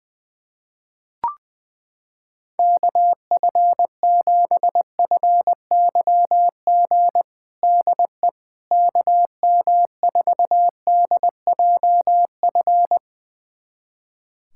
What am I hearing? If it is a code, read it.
Morse code, heard as a keyed tone, about 21 wpm, 700 Hz. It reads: KF7FYG DE KM4DJF
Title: USA Callsign DE Callsign 20wpm